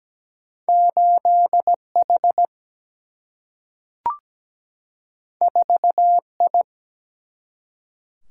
Morse code, heard as a keyed tone, about 17 wpm, 700 Hz. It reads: 8H 4I
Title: Single-Letter Single-Number 17wpm